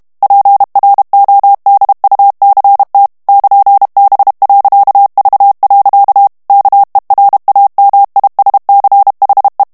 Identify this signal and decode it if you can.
Morse code, heard as a keyed tone, about 32 wpm, 800 Hz. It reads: PRODUCT <KN>B.V. KERAMISCHE
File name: HOLLANDAIS texte clair 2 vitesse 30 mots minute